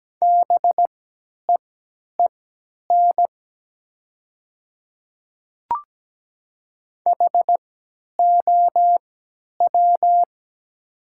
Morse code, heard as a keyed tone, about 17 wpm, spaced as for 10 wpm, 700 Hz. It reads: BEEN HOW
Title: 3x Character Spacing - Top 100 Words - 17wpm